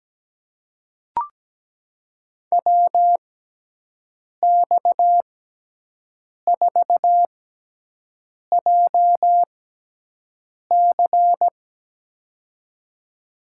Morse code, heard as a keyed tone, about 17 wpm, spaced as for 6 wpm, 700 Hz. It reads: WX4JC
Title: USA Call Signs - 5 Characters - 6x Character Spacing - 17wpm